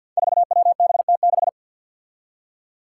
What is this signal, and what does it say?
Morse code, heard as a keyed tone, about 53 wpm, 700 Hz. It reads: 4WBT6
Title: Five Challenging Letter-Number Combinations 50wpm